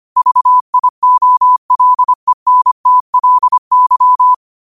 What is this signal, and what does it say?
Morse code, heard as a keyed tone, about 25 wpm, 1000 Hz. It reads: VIOLENTLY